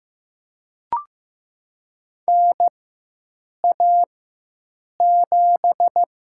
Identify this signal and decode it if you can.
Morse code, heard as a keyed tone, about 15 wpm, 700 Hz. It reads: N A 7
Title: USA Call Signs - 4 Characters - 4x Character Spacing - 15wpm